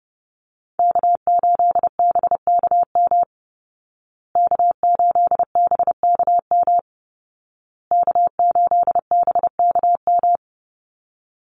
Morse code, heard as a keyed tone, about 30 wpm, 700 Hz. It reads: K8BKM K8BKM K8BKM
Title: Call Signs Repeated 2x - 30wpm